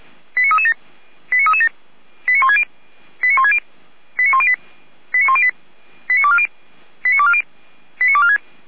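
Five-tone selective call (ZVEI), heard as quick runs of five tones, tones of about 70 ms. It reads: 89337 89337 89160 89160 89108 89108 89240 89240 89246
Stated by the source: ZVEI five-tone selective call received by SDR.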